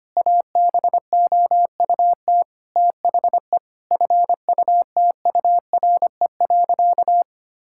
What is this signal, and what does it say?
Morse code, heard as a keyed tone, about 25 wpm, 700 Hz. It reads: ABOUT THE FUTURE.
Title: Sentences from Top 1000 Words - Review Part 5 - 25wpm